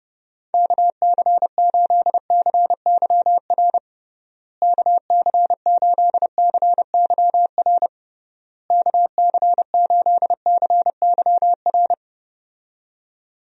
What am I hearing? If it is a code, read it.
Morse code, heard as a keyed tone, about 29 wpm, 700 Hz. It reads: KC8CYR KC8CYR KC8CYR
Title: USA Call Signs - 6 Characters - Repeated 3x - 30wpm